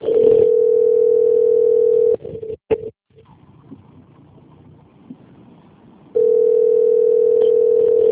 Ringback tone (North America), 440 + 480 Hz.
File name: rg-1600-unknown- - - .94084